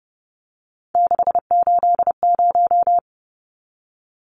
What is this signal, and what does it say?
Morse code, heard as a keyed tone, about 30 wpm, 700 Hz. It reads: 680
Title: Numbers - 3-Digit 30wpm